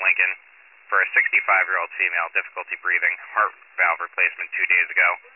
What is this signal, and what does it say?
Emergency Alert System (SAME) tones, heard as harsh buzzing bursts, with speech. It is heard from 0.9 to 5.0 s.